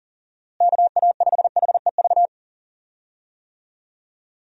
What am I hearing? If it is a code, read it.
Morse code, heard as a keyed tone, about 39 wpm, 700 Hz. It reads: KA5HEV
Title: USA Call Signs - 6 Characters - Repeated 3x - 40wpm